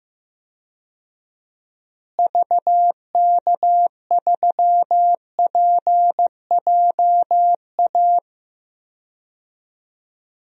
Morse code, heard as a keyed tone, about 15 wpm, 700 Hz. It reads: VK3PJA
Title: DXCC-100 Call Signs - 6 Characters - Repeated 2x - 15wpm